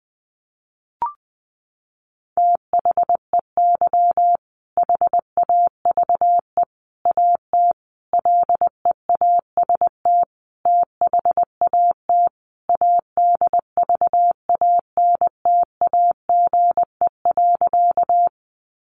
Morse code, heard as a keyed tone, about 20 wpm, 700 Hz. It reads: THEY HAVE AT LEAST THAT ADVANTAGE.
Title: Sentences from Top 900 Words 20wpm